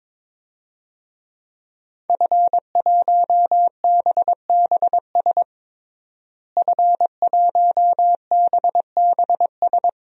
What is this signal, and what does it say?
Morse code, heard as a keyed tone, about 22 wpm, 700 Hz. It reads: F1BBS F1BBS
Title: DXCC-100 Call Signs - 5 Characters - Repeated 3x - 22wpm